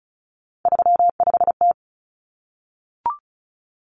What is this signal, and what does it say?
Morse code, heard as a keyed tone, about 34 wpm, 700 Hz. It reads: 35T
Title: Common Cut Numbers - 3-Digit 35wpm